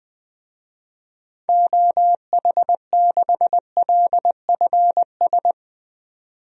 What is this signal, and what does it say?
Morse code, heard as a keyed tone, about 20 wpm, 700 Hz. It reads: OH6LFS
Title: DXCC-100 Call Signs - 6 Characters - 20wpm